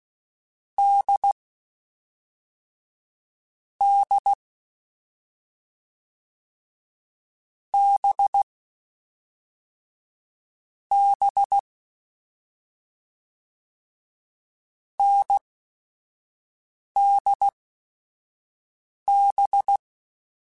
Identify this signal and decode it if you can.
Morse code, heard as a keyed tone, about 16 wpm, spaced as for 5 wpm, 800 Hz. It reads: D D B B NDB
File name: Short Runs / Short Runs-mp3s / awj1ndb6gz7 @5